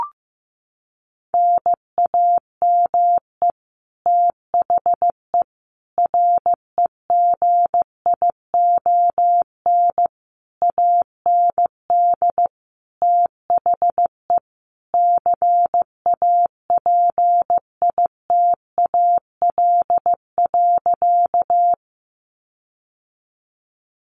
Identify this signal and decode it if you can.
Morse code, heard as a keyed tone, about 15 wpm, 700 Hz. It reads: NAME THE REGION AND THE CAPITAL.
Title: Sentences from Top 1500 Words 15wpm